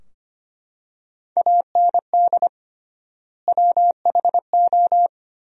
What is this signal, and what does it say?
Morse code, heard as a keyed tone, about 25 wpm, 700 Hz. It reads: AND WHO
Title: 3x Word-Spacing - Sentences from Top 200 Words - 25wpm